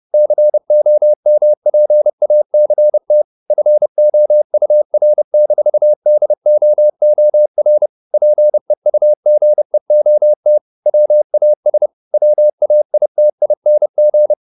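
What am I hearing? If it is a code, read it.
Morse code, heard as a keyed tone, about 30 wpm, 600 Hz. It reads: COMPACT FOUR-DOOR PEUGEOT WAS WAITING
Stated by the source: Morse code at 30 wpm, the novel Never by Ken Follett in English, from CW med Gnister.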